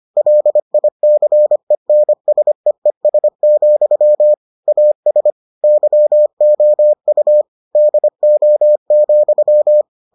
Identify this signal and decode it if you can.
Morse code, heard as a keyed tone, about 25 wpm, 600 Hz. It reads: LICENSEES, AS YOU DO,